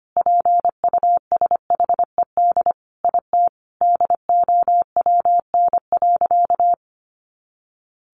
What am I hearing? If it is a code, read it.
Morse code, heard as a keyed tone, about 25 wpm, 700 Hz. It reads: PUSHED IT DOWN.
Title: A Sound Of Thunder 25wpm